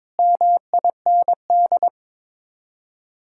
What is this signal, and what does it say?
Morse code, heard as a keyed tone, about 22 wpm, 700 Hz. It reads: MIND
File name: practice / 22wpm / Sets of 7 Words - 22wpm